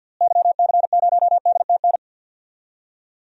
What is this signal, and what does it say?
Morse code, heard as a keyed tone, about 47 wpm, 700 Hz. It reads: YC0DTN
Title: DXCC-100 Call Signs - 6 Characters - Repeated 2x - 50wpm